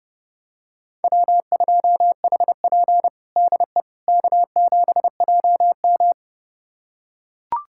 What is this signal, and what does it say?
Morse code, heard as a keyed tone, about 30 wpm, 700 Hz. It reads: W2HP DE K7JM
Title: USA Callsign DE Callsign - 4 Characters - 30wpm